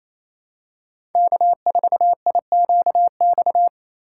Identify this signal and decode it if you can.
Morse code, heard as a keyed tone, about 28 wpm, 700 Hz. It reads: K4IQX
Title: USA Call Signs - 6x Character Spacing - 28wpm